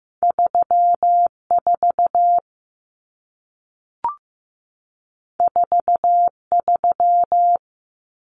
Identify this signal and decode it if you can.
Morse code, heard as a keyed tone, about 15 wpm, 700 Hz. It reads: 34 43
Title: Numbers - 2-Digit 15wpm